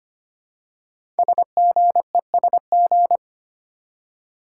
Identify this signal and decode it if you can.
Morse code, heard as a keyed tone, about 25 wpm, 700 Hz. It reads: SGESG
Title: Code Groups - 5 Letters - 25wpm